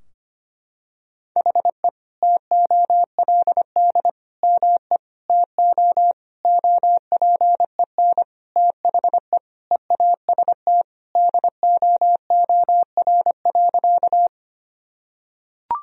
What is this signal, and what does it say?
Morse code, heard as a keyed tone, about 25 wpm, 700 Hz. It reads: HE TOLD ME TO OPEN THE EAST DOOR.
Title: Sentences from Top 1000 Words - Review Part 3 - 25wpm